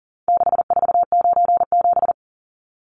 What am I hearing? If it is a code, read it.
Morse code, heard as a keyed tone, about 40 wpm, 700 Hz. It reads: 6497
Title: Numbers - 4-Digit 40wpm